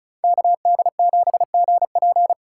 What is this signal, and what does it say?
Morse code, heard as a keyed tone, about 36 wpm, 700 Hz. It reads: KD7GP
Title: Speed Racing - USA 2x2 Call Signs 35wpm